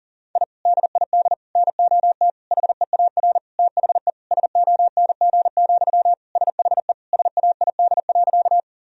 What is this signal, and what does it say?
Morse code, heard as a keyed tone, about 37 wpm, 700 Hz. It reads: I DID NOT HEAR THE SONG, SHE SAID.
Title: Sentences from Top 1000 Words - Review Part 2 - 40wpm